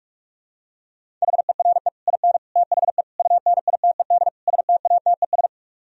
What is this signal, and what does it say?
Morse code, heard as a keyed tone, about 45 wpm, 700 Hz. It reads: HERE IN THE UNITED STATES